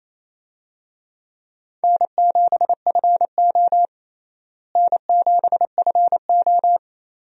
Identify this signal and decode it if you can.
Morse code, heard as a keyed tone, about 28 wpm, 700 Hz. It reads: N7FO N7FO
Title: USA Call Signs - 4 Characters - Repeated 3x - 28wpm